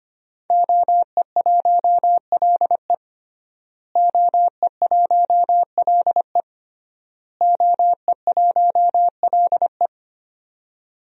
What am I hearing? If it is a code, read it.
Morse code, heard as a keyed tone, about 25 wpm, 700 Hz. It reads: OE1LE OE1LE OE1LE
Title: DXCC-100 Call Signs - 5 Characters - Repeated 3x - 25wpm